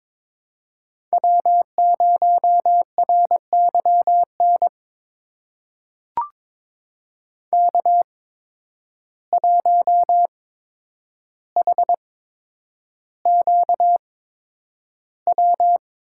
Morse code, heard as a keyed tone, about 22 wpm, spaced as for 9 wpm, 700 Hz. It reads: W0RYN K1HQW
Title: USA Call Signs - 5 Characters - 8x Character Spacing - 22wpm